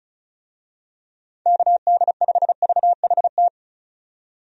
Morse code, heard as a keyed tone, about 36 wpm, 700 Hz. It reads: KD5VHT
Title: USA Call Signs - 6 Characters - 35wpm